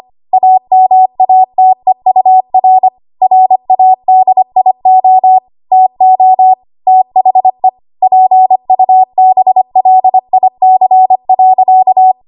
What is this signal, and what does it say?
Morse code, heard as a keyed tone, about 25 wpm, 750 Hz. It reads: AMATEUR RADIO TO THE PUBLIC.
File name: Morse / 25WPM